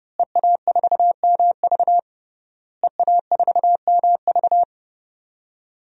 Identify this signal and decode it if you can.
Morse code, heard as a keyed tone, about 30 wpm, 700 Hz. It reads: EA4MV EA4MV
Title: DXCC-100 Call Signs - 5 Characters - Repeated 3x - 30wpm